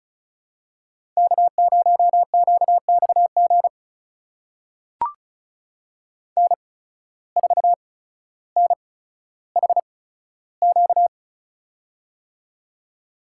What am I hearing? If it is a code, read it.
Morse code, heard as a keyed tone, about 36 wpm, spaced as for 15 wpm, 700 Hz. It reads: K0QXG N4NHQ
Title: USA Call Signs - 8x Character Spacing - 35wpm